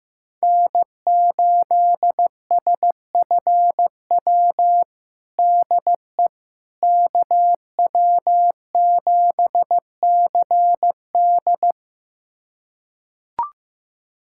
Morse code, heard as a keyed tone, about 15 wpm, 700 Hz. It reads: N8SFW DE KW7CD
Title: USA Callsign DE Callsign - 5 Characters - 15wpm